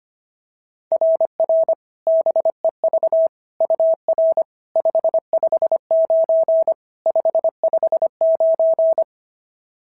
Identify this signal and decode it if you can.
Morse code, heard as a keyed tone, about 25 wpm, 660 Hz. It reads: RR BEV UR 559 559
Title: SOTA QSO Practice 25wpm